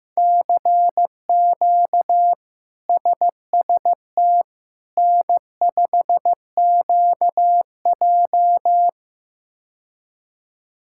Morse code, heard as a keyed tone, about 15 wpm, 700 Hz. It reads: CQ SST N5QJ
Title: K1USN SST QSO Practice - 15wpm